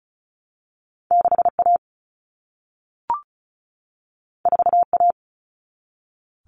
Morse code, heard as a keyed tone, about 36 wpm, 700 Hz. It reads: 6A 4A
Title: Common Cut Numbers - 2-Digit 35wpm